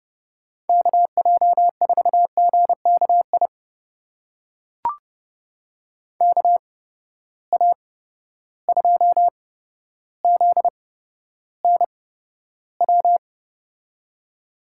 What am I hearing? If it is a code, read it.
Morse code, heard as a keyed tone, about 31 wpm, spaced as for 13 wpm, 700 Hz. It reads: KJ4GKI KA2ZNW
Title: USA Call Signs - 6 Characters - 8x Character Spacing - 30wpm